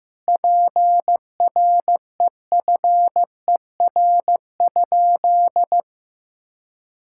Morse code, heard as a keyed tone, about 15 wpm, 700 Hz. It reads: PREFER?
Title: Sentences from Top 1600 Words 15wpm